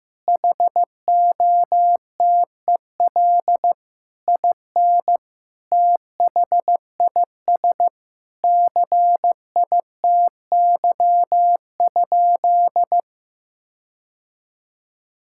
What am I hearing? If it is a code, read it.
Morse code, heard as a keyed tone, about 15 wpm, 700 Hz. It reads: HOTEL IN THIS CITY?
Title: Sentences from Top 1600 Words 15wpm